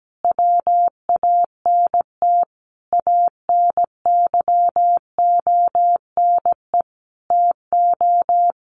Morse code, heard as a keyed tone, about 17 wpm, 700 Hz. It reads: WANT ANYONE TO
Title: Sentences from Top 1100 Words 17wpm